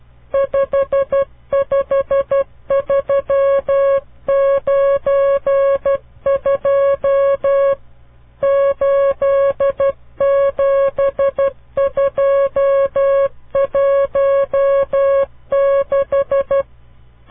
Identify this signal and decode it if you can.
Morse code, heard as a keyed tone, about 12 wpm, 550 Hz. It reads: 55392 87216
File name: CW Learning Programs / farnsworth / unit 5